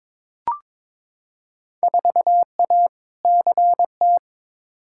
Morse code, heard as a keyed tone, about 22 wpm, 700 Hz. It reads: 4A CT
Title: Field Day Exchanges Encore - 22wpm